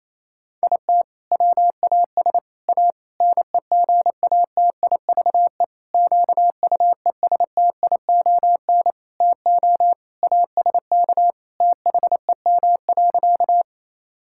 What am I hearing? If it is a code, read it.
Morse code, heard as a keyed tone, about 28 wpm, 700 Hz. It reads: IT WAS A NEGATIVE QUESTION TO ASK THEM.
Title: Sentences from Top 1000 Words - Review Part 6 - 28wpm